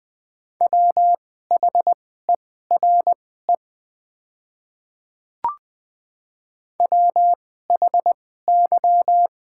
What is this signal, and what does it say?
Morse code, heard as a keyed tone, about 20 wpm, spaced as for 14 wpm, 700 Hz. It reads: WHERE WHY
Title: 2x Character Spacing - Top 300 Words - 20wpm